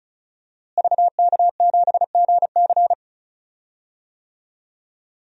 Morse code, heard as a keyed tone, about 36 wpm, 700 Hz. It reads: VK7GC
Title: DXCC-100 Call Signs - 5 Characters - Repeated 3x - 35wpm